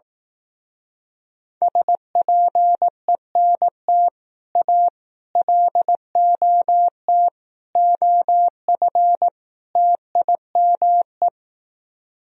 Morse code, heard as a keyed tone, about 18 wpm, 700 Hz. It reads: SPENT A LOT OF TIME
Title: Speed-Racing - Sets of 5 Words 15wpm